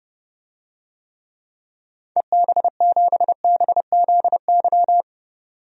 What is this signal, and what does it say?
Morse code, heard as a keyed tone, about 30 wpm, 700 Hz. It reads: EB7BZY
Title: DXCC-100 Call Signs - 6 Characters - 30wpm